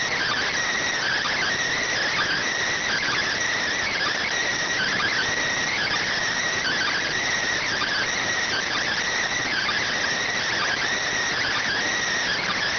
Emergency Alert System (SAME) tones (header bursts), heard as harsh buzzing bursts. It is heard from 1.0 to 11.8 s.